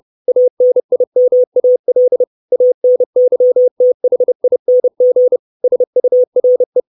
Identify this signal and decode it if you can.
Morse code, heard as a keyed tone, about 30 wpm, 500 Hz. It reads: ANIMAL ANYTHING SURE
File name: 632 Words 30WPM